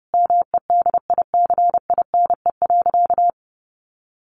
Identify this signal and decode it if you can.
Morse code, heard as a keyed tone, about 30 wpm, 700 Hz. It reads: MEDICINE.
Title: Sentences from Top 900 Words 30wpm